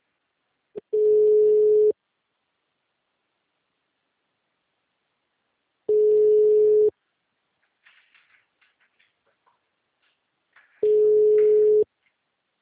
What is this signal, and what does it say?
Ringback tone (Europe, 425 Hz).
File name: out- -2036- - - .37574